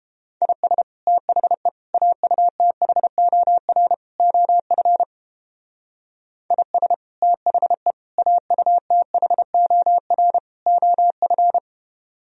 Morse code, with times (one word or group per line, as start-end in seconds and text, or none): 0.4-0.8 IS
1.1-1.7 THE
1.9-3.9 AUTHOR
4.2-5.0 OF
6.5-6.9 IS
7.2-7.9 THE
8.2-10.4 AUTHOR
10.7-11.6 OF